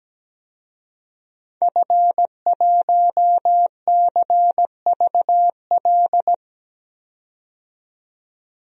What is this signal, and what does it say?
Morse code, heard as a keyed tone, about 17 wpm, 700 Hz. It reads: F1CVL
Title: DXCC-100 Call Signs - 5 Characters - 17wpm